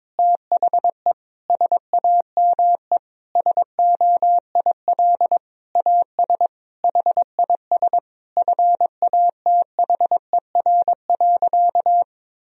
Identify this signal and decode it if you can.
Morse code, heard as a keyed tone, about 22 wpm, 700 Hz. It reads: THE SAME SOIL AS HIS FATHER.
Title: Sentences from Top 1000 Words - Review Part 4 - 22wpm